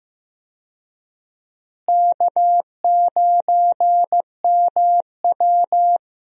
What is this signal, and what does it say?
Morse code, heard as a keyed tone, about 15 wpm, 700 Hz. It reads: K9MW
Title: USA Call Signs - 4 Characters - Repeated 2x - 15wpm